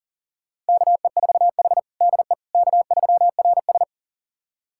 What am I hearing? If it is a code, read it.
Morse code, heard as a keyed tone, about 37 wpm, 690 Hz. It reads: KE4H DE K3RS
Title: USA Callsign DE Callsign - 4 Characters - 40wpm